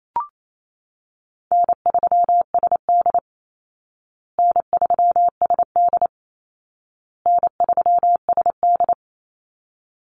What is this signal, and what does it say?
Morse code, heard as a keyed tone, about 28 wpm, 700 Hz. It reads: N3SD N3SD N3SD
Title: Easy Call Signs Repeated 3x - 28wpm